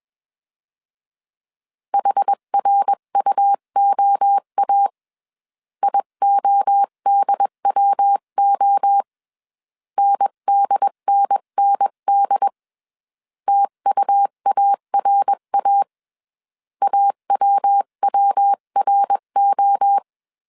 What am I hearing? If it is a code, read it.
Morse code, heard as a keyed tone, about 21 wpm, 800 Hz. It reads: HRUOA IODWO NDNND TUARA AWWRO
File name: 20wpm